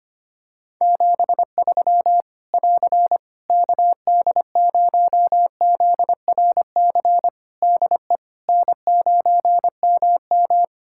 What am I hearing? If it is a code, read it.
Morse code, heard as a keyed tone, about 25 wpm, 700 Hz. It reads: 73 <AR> KD0ZRC DE N9MM
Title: Practice QSO - Exchange 4 - Ending - 25wpm